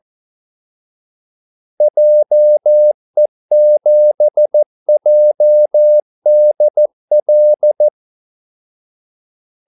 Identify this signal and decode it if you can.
Morse code, heard as a keyed tone, about 14 wpm, 600 Hz. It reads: JE7JDL
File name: Callsigns-14wpm